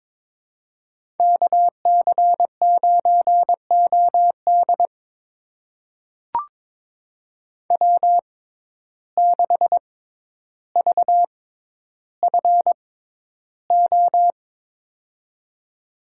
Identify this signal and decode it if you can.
Morse code, heard as a keyed tone, about 22 wpm, spaced as for 11 wpm, 700 Hz. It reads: KC9OD W6VFO
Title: USA Call Signs - 5 Characters - 6x Character Spacing - 22wpm